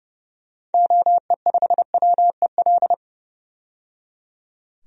Morse code, heard as a keyed tone, about 31 wpm, 700 Hz. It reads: OE5WEL
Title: DXCC-100 Call Signs - 6 Characters - 30wpm